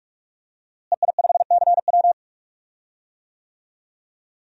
Morse code, heard as a keyed tone, about 46 wpm, 700 Hz. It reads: EI5CW
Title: DXCC-100 Call Signs - 5 Characters - Repeated 2x - 45wpm